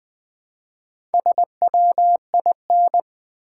Morse code, heard as a keyed tone, about 20 wpm, 700 Hz. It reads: SWIN
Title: Build A Word - 1 Letter At A Time - 5-Letter Words - 20wpm